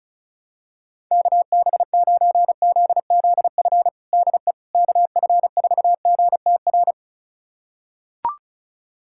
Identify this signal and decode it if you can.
Morse code, heard as a keyed tone, about 36 wpm, 700 Hz. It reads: KB9ZZF DE KF4GTR
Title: USA Callsign DE Callsign - 6 Characters - 35wpm